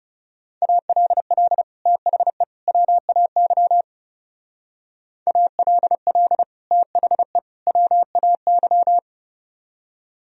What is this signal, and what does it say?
Morse code, heard as a keyed tone, about 39 wpm, 700 Hz. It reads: ALL THE WAY ALL THE WAY